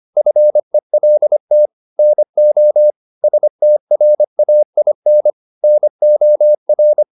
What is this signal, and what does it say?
Morse code, heard as a keyed tone, about 25 wpm, 600 Hz. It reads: FELT NO STRAIN NOR
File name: the old man and the sea part 1 25wpm